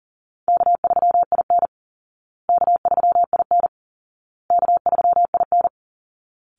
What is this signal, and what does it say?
Morse code, heard as a keyed tone, about 40 wpm, 700 Hz. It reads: K3IN K3IN K3IN
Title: Easy Call Signs Repeated 3x - 40wpm